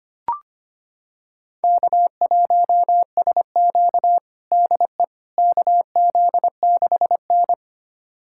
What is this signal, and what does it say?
Morse code, heard as a keyed tone, about 25 wpm, 700 Hz. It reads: K1SQ DE KZ6N
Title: USA Callsign DE Callsign - 4 Characters - 25wpm